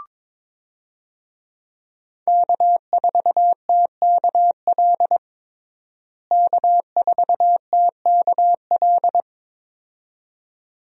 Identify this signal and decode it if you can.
Morse code, heard as a keyed tone, about 22 wpm, 700 Hz. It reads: K4TKL K4TKL
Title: USA Call Signs - 5 Characters - Repeated 2x - 22wpm